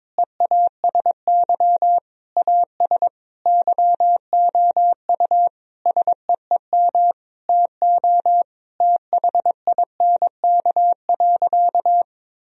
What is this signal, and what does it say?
Morse code, heard as a keyed tone, about 22 wpm, 700 Hz. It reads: EASY AS YOU SEEM TO THINK.
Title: Sentences from Top 1000 Words - Review Part 1 - 22wpm